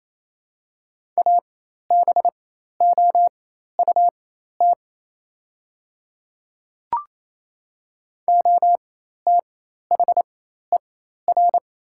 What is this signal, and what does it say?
Morse code, heard as a keyed tone, about 28 wpm, spaced as for 13 wpm, 700 Hz. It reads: ABOUT OTHER
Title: 4x Character Spacing - Top 100 Words - 28wpm